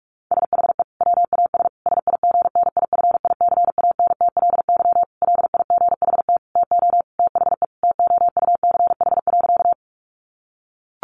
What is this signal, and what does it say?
Morse code, heard as a keyed tone, about 44 wpm, 700 Hz. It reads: SHE WAS SIGNIFICANTLY LIGHT TO THE TOUCH.
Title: Sentences from Top 1500 Words 45wpm